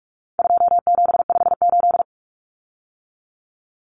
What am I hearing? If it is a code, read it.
Morse code, heard as a keyed tone, about 44 wpm, 700 Hz. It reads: 2758
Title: Numbers - 4-Digit 45wpm